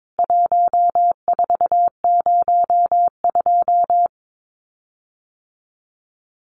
Morse code, heard as a keyed tone, about 22 wpm, 700 Hz. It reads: 1402
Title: CWT Exchange 22wpm